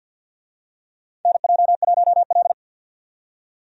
Morse code, heard as a keyed tone, about 45 wpm, 700 Hz. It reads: NJ1L
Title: USA Call Signs - 4 Characters - 50wpm